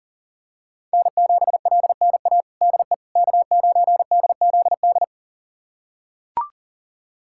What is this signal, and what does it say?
Morse code, heard as a keyed tone, about 40 wpm, 700 Hz. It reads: N7LNA DE K9DZD
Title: USA Callsign DE Callsign - 5 Characters - 40wpm